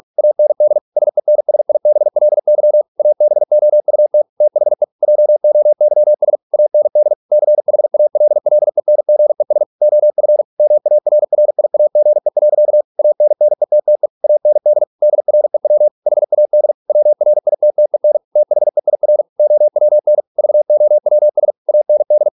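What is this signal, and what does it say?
Morse code, heard as a keyed tone, about 48 wpm, 600 Hz. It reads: AND SENSIBLY ABOUT THE JOYS AND CHALLENGES OF MARRIAGE. ANNETTE AND DREW HAD WRITTEN THEIR OWN VOWS AND
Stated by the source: Morse practice at 45 wpm, the novel Never by Ken Follett in English, from CW med Gnister.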